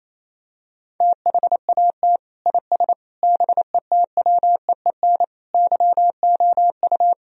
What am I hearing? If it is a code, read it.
Morse code, heard as a keyed tone, about 28 wpm, 700 Hz. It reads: THAT IS BETWEEN YOU
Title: Sentences from Top 1000 Words - Review Part 4 - 28wpm